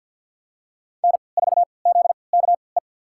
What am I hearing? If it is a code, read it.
Morse code, heard as a keyed tone, about 53 wpm, spaced as for 29 wpm, 700 Hz. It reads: N4ZKE
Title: USA Call Signs - 5 Characters - 3x Character Spacing - 50wpm